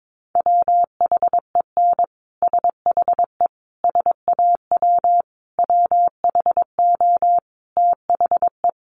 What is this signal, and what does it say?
Morse code, heard as a keyed tone, about 22 wpm, 700 Hz. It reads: WHEN SHE SAW WHO THE
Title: Sentences from Top 1200 Words 22wpm